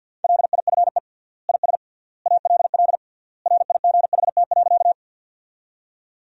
Morse code, heard as a keyed tone, about 47 wpm, 700 Hz. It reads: LIFE IS ALL RIGHT.
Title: 3x Word-Spacing - Sentences from Top 300 Words - 50wpm